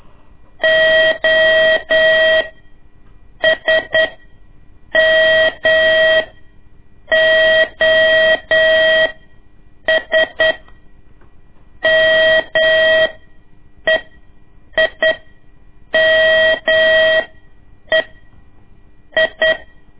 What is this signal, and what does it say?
Morse code, heard as a keyed tone, about 9 wpm, spaced as for 7 wpm, 640 Hz. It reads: O S M OS MEIME I
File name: ik1weg2 / CORSI CW / lez2-1